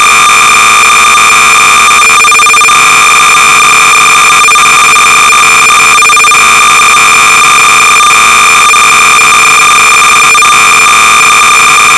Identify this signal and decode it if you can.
Cassette data (Kansas City type).
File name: sorcerer / 300baud / Galaxians(another version)-Sorcerer-300baud